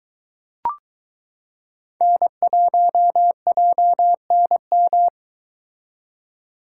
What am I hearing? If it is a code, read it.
Morse code, heard as a keyed tone, about 23 wpm, 700 Hz. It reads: N1JNM